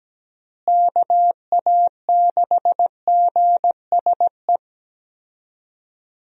Morse code, heard as a keyed tone, about 17 wpm, 700 Hz. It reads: KA6GSE
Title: USA Call Signs - 6 Characters - Repeated 3x - 17wpm